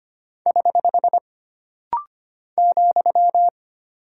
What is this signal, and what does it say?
Morse code, heard as a keyed tone, about 25 wpm, 700 Hz. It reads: <HH> ,
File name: Single Punctuation-Prosign - Rapid-Fire - 25wpm